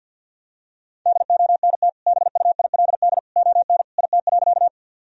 Morse code, heard as a keyed tone, about 50 wpm, 700 Hz. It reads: DONT BUILD ON IT.
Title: Sentences from Top 1000 Words - Review Part 1 - 50wpm